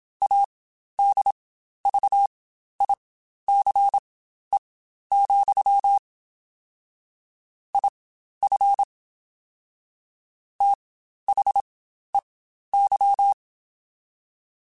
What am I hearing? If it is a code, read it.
Morse code, heard as a keyed tone, about 27 wpm, spaced as for 12 wpm, 800 Hz. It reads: ADVICE, IF THEY
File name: Quotes Spaced / Quotes Spaced-mp3s / Advice sp @25